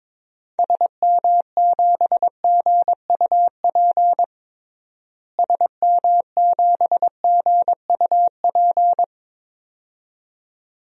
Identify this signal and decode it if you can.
Morse code, heard as a keyed tone, about 22 wpm, 700 Hz. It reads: SM7GUP SM7GUP
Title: DXCC-100 Call Signs - 6 Characters - Repeated 2x - 22wpm